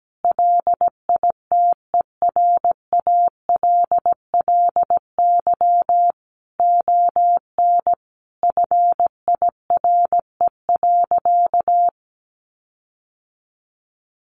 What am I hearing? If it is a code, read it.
Morse code, heard as a keyed tone, about 17 wpm, 700 Hz. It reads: LITERALLY ON FIRE.
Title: Sentences from Top 1400 Words 17wpm